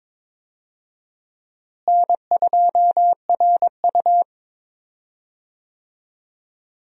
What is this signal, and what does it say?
Morse code, heard as a keyed tone, about 22 wpm, 700 Hz. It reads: N2RU
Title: USA Call Signs - 4 Characters - Repeated 3x - 22wpm